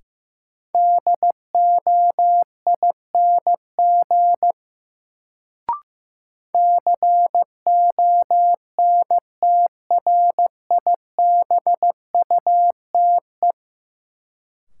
Morse code, heard as a keyed tone, about 15 wpm, 700 Hz. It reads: DOING CONTRIBUTE
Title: Sentences from Top 1400 Words - New Word Drill - 15wpm